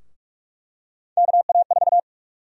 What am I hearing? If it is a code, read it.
Morse code, heard as a keyed tone, about 44 wpm, 700 Hz. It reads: KA4
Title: USA Call Sign Prefixes - Repeated 2x - 45wpm